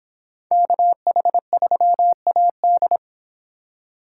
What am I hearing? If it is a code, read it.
Morse code, heard as a keyed tone, about 26 wpm, 700 Hz. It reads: KH3AD